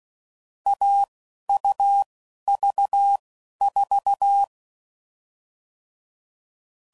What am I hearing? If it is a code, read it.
Morse code, heard as a keyed tone, about 16 wpm, spaced as for 11 wpm, 800 Hz. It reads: AUV4
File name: Combinations / Difficult Combos / Difficult Combos